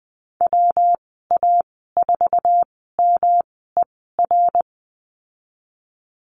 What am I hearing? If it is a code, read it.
Morse code, heard as a keyed tone, about 20 wpm, spaced as for 15 wpm, 700 Hz. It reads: WA4MER
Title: 173 Lesson 28g - Number 9 - Call Signs 20wpm